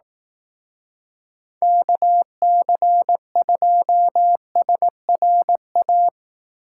Morse code, heard as a keyed tone, about 19 wpm, 700 Hz. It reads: KC2SRA
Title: Speed Racing - USA 2x3 Call Signs 15wpm